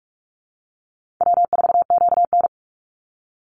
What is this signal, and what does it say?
Morse code, heard as a keyed tone, about 44 wpm, 700 Hz. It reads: W4QN
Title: Easy Call Signs Repeated 3x - 45wpm